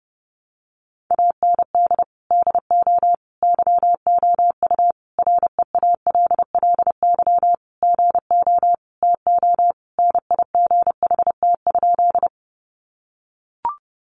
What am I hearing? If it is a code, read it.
Morse code, heard as a keyed tone, about 30 wpm, 700 Hz. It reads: AND DO YOU REALLY GO TONIGHT?
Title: Sentences from Top 600 Words 30wpm